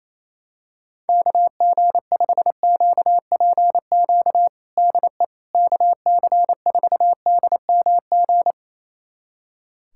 Morse code, heard as a keyed tone, about 28 wpm, 700 Hz. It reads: KG5QPQ DE KC4DMG
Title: USA Callsign DE Callsign - 6 Characters - 28wpm